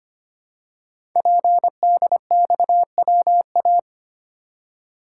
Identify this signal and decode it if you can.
Morse code, heard as a keyed tone, about 25 wpm, 700 Hz. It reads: PDXWA
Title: Code Groups - 5 Letter:Numbers - 25wpm